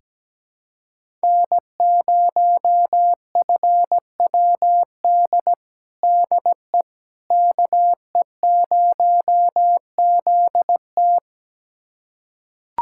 Morse code, heard as a keyed tone, about 17 wpm, 700 Hz. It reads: N0FWD DE KE0ZT
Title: USA Callsign DE Callsign - 5 Characters - 17wpm